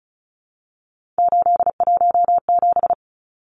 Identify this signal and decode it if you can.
Morse code, heard as a keyed tone, about 34 wpm, 700 Hz. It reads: 817
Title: Numbers - 3-Digit 35wpm